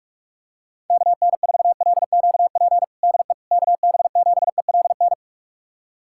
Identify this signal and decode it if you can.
Morse code, heard as a keyed tone, about 46 wpm, 700 Hz. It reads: KN4LQP DE KB7ELN
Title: USA Callsign DE Callsign 45wpm